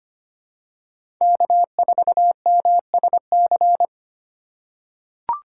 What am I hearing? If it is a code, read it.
Morse code, heard as a keyed tone, about 25 wpm, 700 Hz. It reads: K4MSC
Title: USA Call Signs - 5 Characters - 2x Character Spacing - 25wpm